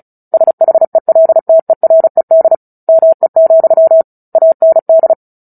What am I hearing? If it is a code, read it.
Morse code, heard as a keyed tone, about 34 wpm, 650 Hz. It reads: SHELTERED ME, AND
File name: Book Two / 09-Chapter 9 - 35 WPM